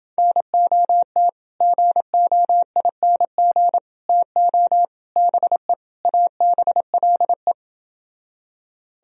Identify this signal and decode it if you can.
Morse code, heard as a keyed tone, about 28 wpm, 700 Hz. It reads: NOT GOING TO BE ABLE